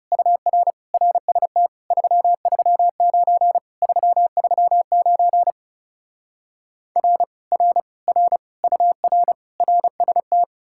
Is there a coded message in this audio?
Yes: Morse code from 0.1 to 10.4 s, heard as a keyed tone, about 40 wpm, 700 Hz. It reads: UR RST 339 339 R R R UR RST